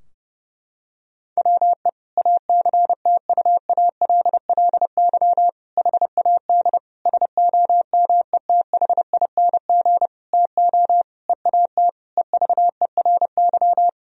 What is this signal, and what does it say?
Morse code, heard as a keyed tone, about 30 wpm, 700 Hz. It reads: WE ACTUALLY HAD SOMETHING TO EAT EVERY
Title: Sentences from Top 1000 Words - Review Part 4 - 30wpm